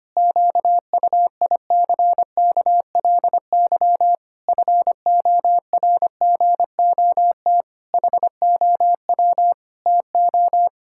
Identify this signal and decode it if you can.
Morse code, heard as a keyed tone, about 25 wpm, 700 Hz. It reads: QUICKLY FORGOT HOW TO
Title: Pangram Challenge v2 - 25wpm